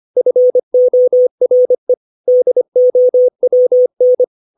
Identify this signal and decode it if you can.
Morse code, heard as a keyed tone, about 25 wpm, 500 Hz. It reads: FORE DOWN
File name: 632 words / 632 Words 25WPM